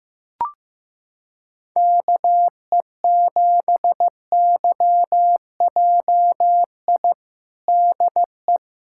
Morse code, heard as a keyed tone, about 16 wpm, 700 Hz. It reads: KE7YJI DE
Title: USA Callsign DE Callsign - 6 Characters - 15wpm